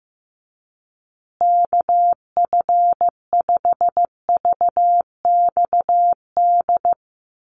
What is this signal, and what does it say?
Morse code, heard as a keyed tone, about 15 wpm, 700 Hz. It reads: KF5VXD
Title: USA Call Signs Repeated 3x 15wpm